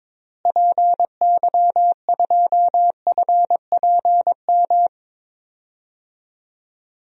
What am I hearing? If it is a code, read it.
Morse code, heard as a keyed tone, about 22 wpm, 700 Hz. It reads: PY2FPM
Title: DXCC-100 Call Signs - 6 Characters - Repeated 2x - 22wpm